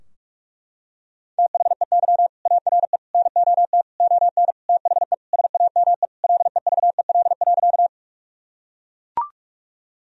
Morse code, heard as a keyed tone, about 41 wpm, 700 Hz. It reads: THEY ARE NOT ON THE SAME LEVEL.
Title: Sentences from Top 1000 Words - Review Part 4 - 45wpm